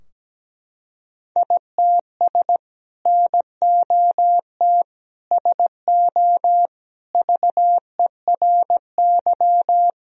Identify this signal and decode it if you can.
Morse code, heard as a keyed tone, about 17 wpm, 700 Hz. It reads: ITS NOT SO VERY
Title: Sentences from Top 1000 Words - Review Part 3 - 17wpm